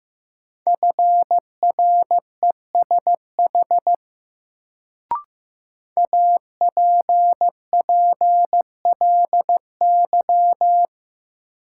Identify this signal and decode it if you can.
Morse code, heard as a keyed tone, about 15 wpm, 700 Hz. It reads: FRESH APPLY
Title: Sentences from Top 1200 Words - New Word Drill Alt - 15wpm